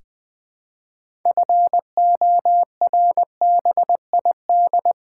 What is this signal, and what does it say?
Morse code, heard as a keyed tone, about 20 wpm, 700 Hz. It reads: FORBID
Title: 6 Letter Words - First 3 Letters Given - 20wpm